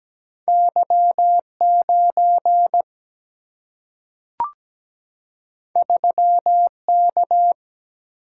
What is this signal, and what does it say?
Morse code, heard as a keyed tone, about 17 wpm, 700 Hz. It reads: Y 9 3 K
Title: Single-Letter Single-Number 17wpm